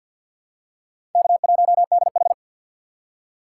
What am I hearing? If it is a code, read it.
Morse code, heard as a keyed tone, about 52 wpm, 700 Hz. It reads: K1DH
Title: USA Call Signs - 2x Character Spacing - 50wpm